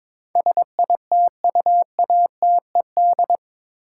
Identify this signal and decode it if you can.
Morse code, heard as a keyed tone, about 21 wpm, 700 Hz. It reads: SITUATED
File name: practice / 22wpm / Eight Letter Words - 22wpm